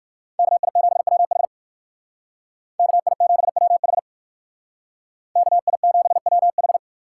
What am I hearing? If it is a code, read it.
Morse code, heard as a keyed tone, about 56 wpm, 700 Hz. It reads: KI7WH KI7WH KI7WH
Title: Speed Racing - USA 2x2 Call Signs 45wpm